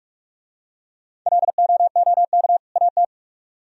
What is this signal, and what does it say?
Morse code, heard as a keyed tone, about 45 wpm, 700 Hz. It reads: LOOK AT